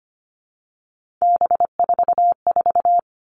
Morse code, heard as a keyed tone, about 25 wpm, 700 Hz. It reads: B44
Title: All Cut Numbers - 3-Digit 25wpm